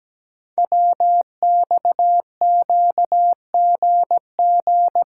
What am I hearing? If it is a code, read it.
Morse code, heard as a keyed tone, about 17 wpm, 700 Hz. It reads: WXQGG
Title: Code Groups - 5 Letters - 17wpm